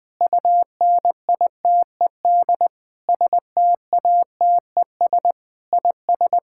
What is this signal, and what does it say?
Morse code, heard as a keyed tone, about 21 wpm, 700 Hz. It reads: UNITED STATES IS
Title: Sentences with 5-Word Phrases - Spoken Beginning - 20wpm